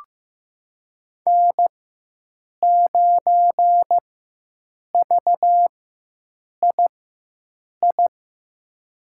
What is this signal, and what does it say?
Morse code, heard as a keyed tone, about 15 wpm, spaced as for 7 wpm, 700 Hz. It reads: N9VII
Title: USA Call Signs - 5 Characters - 4x Character Spacing - 15wpm